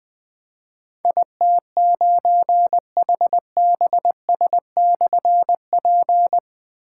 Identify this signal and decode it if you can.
Morse code, heard as a keyed tone, about 20 wpm, 700 Hz. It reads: IT9HBS/P
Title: Difficult Call Signs 20wpm